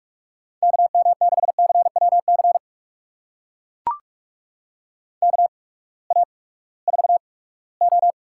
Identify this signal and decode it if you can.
Morse code, heard as a keyed tone, about 44 wpm, spaced as for 22 wpm, 700 Hz. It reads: KM6CWC KA4O
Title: USA Call Signs - 6 Characters - 8x Character Spacing - 45wpm